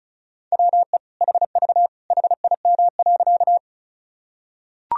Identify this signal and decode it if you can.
Morse code, heard as a keyed tone, about 34 wpm, 700 Hz. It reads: WE HV HIM.
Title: Sentences from Top 100 Words and Abbreviations 35wpm